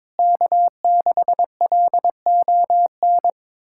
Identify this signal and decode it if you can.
Morse code, heard as a keyed tone, about 22 wpm, 700 Hz. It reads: K6LON
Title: USA Callsign DE Callsign - 5 Characters - 22wpm